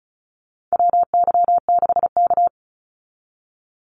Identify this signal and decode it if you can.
Morse code, heard as a keyed tone, about 34 wpm, 700 Hz. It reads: WY6K
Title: Call Signs - 35wpm